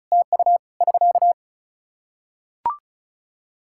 Morse code, heard as a keyed tone, about 34 wpm, 700 Hz. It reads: TU <SK>
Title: Practice QSO - Exchange 4 - Ending - 35wpm